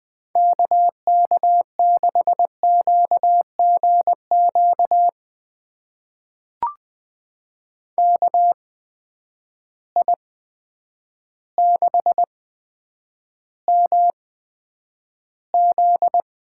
Morse code, heard as a keyed tone, about 20 wpm, spaced as for 9 wpm, 700 Hz. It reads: KK6QGQ KI6MZ
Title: USA Call Signs - 6 Characters - 8x Character Spacing - 20wpm